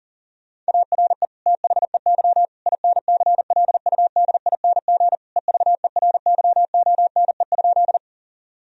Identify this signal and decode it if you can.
Morse code, heard as a keyed tone, about 40 wpm, 700 Hz. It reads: ARE THEY INCLUDING EVERYONE?
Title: Sentences from Top 1000 Words - Review Part 4 - 40wpm